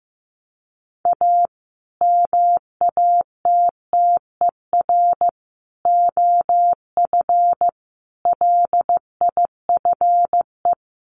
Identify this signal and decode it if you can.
Morse code, heard as a keyed tone, about 14 wpm, 700 Hz. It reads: A MATTER OF LIFE
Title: Sentences from Top 1100 Words 15wpm